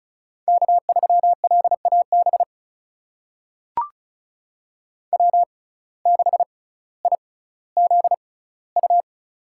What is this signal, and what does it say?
Morse code, heard as a keyed tone, about 36 wpm, spaced as for 18 wpm, 700 Hz. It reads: K3LAB W6IZU
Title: USA Call Signs - 5 Characters - 6x Character Spacing - 35wpm